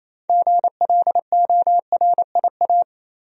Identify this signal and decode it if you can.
Morse code, heard as a keyed tone, about 28 wpm, 700 Hz. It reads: GLORIA
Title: USA Common Names - 28wpm